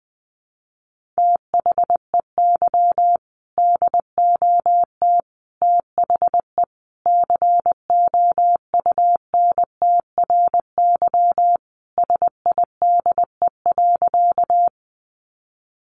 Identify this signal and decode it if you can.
Morse code, heard as a keyed tone, about 20 wpm, 700 Hz. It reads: THEY DOT THE COUNTRYSIDE.
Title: Sentences from Top 1500 Words 20wpm